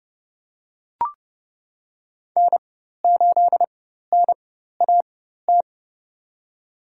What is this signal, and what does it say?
Morse code, heard as a keyed tone, about 31 wpm, spaced as for 14 wpm, 700 Hz. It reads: N8NAT
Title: USA Call Signs - 5 Characters - 4x Character Spacing - 30wpm